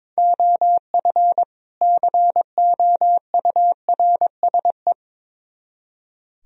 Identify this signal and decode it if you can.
Morse code, heard as a keyed tone, about 23 wpm, 700 Hz. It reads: OF COURSE
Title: Speed-Racing - Sets of 2 Words 22wpm